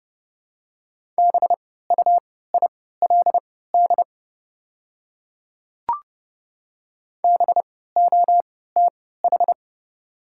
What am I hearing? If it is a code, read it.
Morse code, heard as a keyed tone, about 31 wpm, spaced as for 17 wpm, 690 Hz. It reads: BUILD BOTH
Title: 3x Character Spacing - Top 300 Words - 30wpm